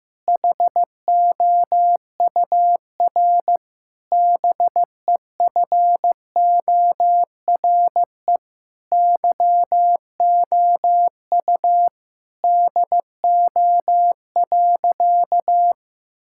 Morse code, heard as a keyed tone, about 15 wpm, 700 Hz. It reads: HOUR BEFORE YOU DO.
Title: Sentences from Top 1000 Words - Review Part 6 - 15wpm